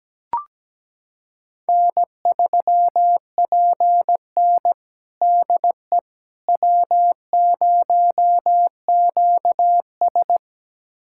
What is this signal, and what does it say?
Morse code, heard as a keyed tone, about 17 wpm, 700 Hz. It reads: N3PN DE W0QS
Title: USA Callsign DE Callsign - 4 Characters - 17wpm